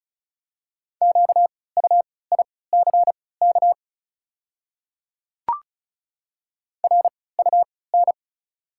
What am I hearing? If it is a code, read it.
Morse code, heard as a keyed tone, about 36 wpm, spaced as for 20 wpm, 690 Hz. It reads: QUICK RUN
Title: 3x Character Spacing - Top 500 Words - 35wpm